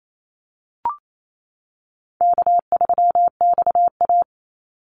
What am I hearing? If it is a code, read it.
Morse code, heard as a keyed tone, about 28 wpm, 700 Hz. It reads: K3XA
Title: Call Signs - 28wpm